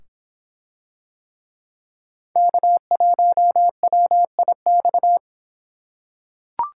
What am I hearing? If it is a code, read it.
Morse code, heard as a keyed tone, about 24 wpm, 700 Hz. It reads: K1WIX